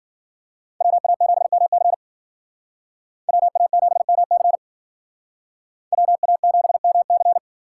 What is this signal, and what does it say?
Morse code, heard as a keyed tone, about 49 wpm, 700 Hz. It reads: WA7MC WA7MC WA7MC
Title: Speed Racing - USA 2x2 Call Signs 40wpm